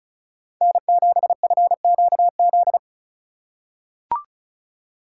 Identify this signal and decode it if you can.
Morse code, heard as a keyed tone, about 36 wpm, 700 Hz. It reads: N7FQZ
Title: USA Call Signs - 5 Characters - 3x Character Spacing - 35wpm